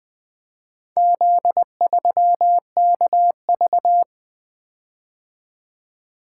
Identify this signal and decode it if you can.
Morse code, heard as a keyed tone, about 20 wpm, 700 Hz. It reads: Z3KV
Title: Four Letter-Number Combinations - 20wpm